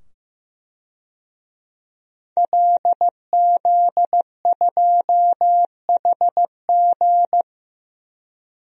Morse code, heard as a keyed tone, about 15 wpm, 700 Hz. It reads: LZ2HG
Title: DXCC-100 Call Signs - 5 Characters - 15wpm